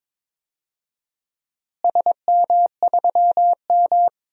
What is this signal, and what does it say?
Morse code, heard as a keyed tone, about 22 wpm, 700 Hz. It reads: SM3M
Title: DXCC-100 Call Signs - 4 Characters - Repeated 3x - 22wpm